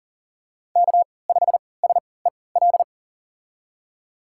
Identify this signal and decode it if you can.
Morse code, heard as a keyed tone, about 38 wpm, spaced as for 22 wpm, 700 Hz. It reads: K5SEL
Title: USA Call Signs - 3x Character Spacing - 40wpm